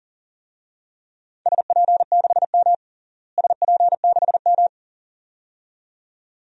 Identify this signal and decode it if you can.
Morse code, heard as a keyed tone, about 40 wpm, 700 Hz. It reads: SP6M SP6M
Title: DXCC-100 Call Signs - 4 Characters - Repeated 2x - 40wpm